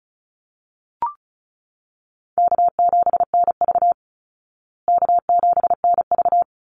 Morse code, heard as a keyed tone, about 34 wpm, 700 Hz. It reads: K7NV K7NV
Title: Easy Call Signs Repeated 2x - 35wpm